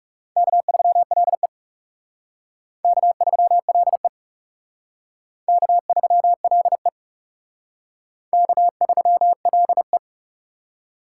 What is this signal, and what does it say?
Morse code, heard as a keyed tone, about 40 wpm, 700 Hz. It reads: K3LE K3LE K3LE K3LE